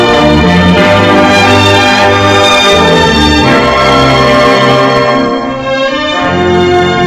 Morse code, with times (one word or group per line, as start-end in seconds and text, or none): none